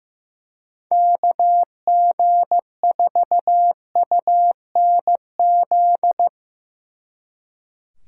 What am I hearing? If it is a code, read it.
Morse code, heard as a keyed tone, about 17 wpm, 700 Hz. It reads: KG4UNZ
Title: Speed Racing - USA Call Signs 15wpm